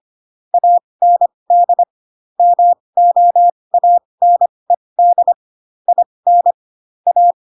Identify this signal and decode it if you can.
Morse code, heard as a keyed tone, about 25 wpm, spaced as for 20 wpm, 700 Hz. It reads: AND MOANED IN A
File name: Pinocchio / Pinocchio ch6 15 25 ik1hge